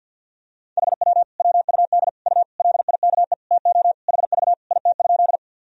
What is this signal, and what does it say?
Morse code, heard as a keyed tone, about 53 wpm, 700 Hz. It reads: HW WUD U LIKE TO HV IT?
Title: Sentences from Top 100 Words and Abbreviations 50wpm